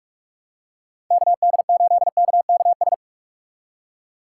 Morse code, heard as a keyed tone, about 43 wpm, 700 Hz. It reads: KD8KKS
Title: USA Call Signs - 6x Character Spacing - 45wpm